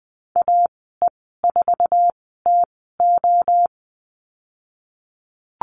Morse code, heard as a keyed tone, about 20 wpm, spaced as for 15 wpm, 700 Hz. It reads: AE4TO
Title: 048 Lesson 9g - Number 4 - Call Signs 20wpm